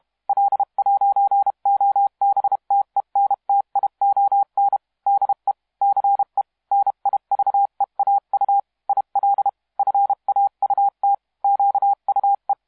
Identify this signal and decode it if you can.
Morse code, heard as a keyed tone, about 31 wpm, 800 Hz. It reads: L'OBTENTION DE CE NIVEAU IL FAUT QUE
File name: musiqueradio / cw hsc demo f6ddr